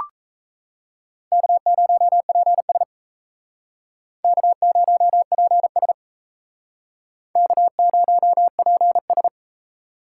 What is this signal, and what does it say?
Morse code, heard as a keyed tone, about 35 wpm, 700 Hz. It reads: K0PS K0PS K0PS
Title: Speed Racing - USA 1x2 Call Signs 28wpm